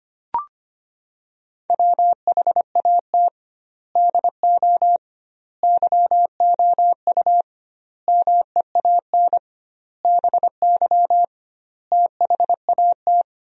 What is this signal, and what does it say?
Morse code, heard as a keyed tone, about 25 wpm, 700 Hz. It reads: WHAT DO YOU MEAN BY THAT
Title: 2x Word-Spacing - Sentences from Top 400 Words - 25wpm